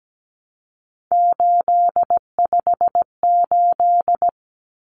Morse code, heard as a keyed tone, about 17 wpm, 700 Hz. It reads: 858
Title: Numbers - 3-Digit 17wpm